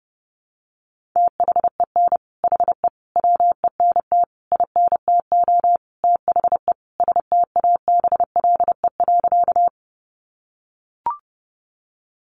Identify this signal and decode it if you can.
Morse code, heard as a keyed tone, about 30 wpm, 700 Hz. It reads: THEN HE WENT INTO THE STABLE.
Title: Sentences from Top 1500 Words 30wpm